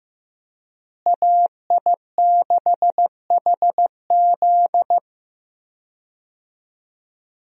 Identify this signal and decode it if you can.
Morse code, heard as a keyed tone, about 15 wpm, 700 Hz. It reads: AI6HZ
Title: USA Call Signs - 5 Characters - Repeated 3x - 15wpm